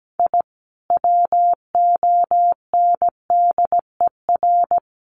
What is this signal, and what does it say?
Morse code, heard as a keyed tone, about 17 wpm, 700 Hz. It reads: I WONDER
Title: Sentences from Top 700 Words 17wpm